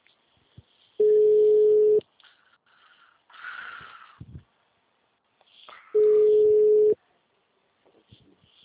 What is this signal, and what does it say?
Ringback tone (Europe, 425 Hz).
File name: out- -2007- - - .442